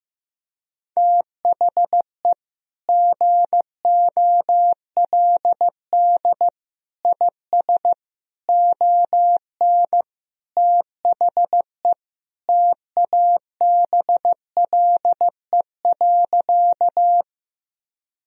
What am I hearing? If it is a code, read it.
Morse code, heard as a keyed tone, about 15 wpm, 700 Hz. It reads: THE GOLD IS ON THE TABLE.
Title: Sentences from Top 1000 Words - Review Part 3 - 15wpm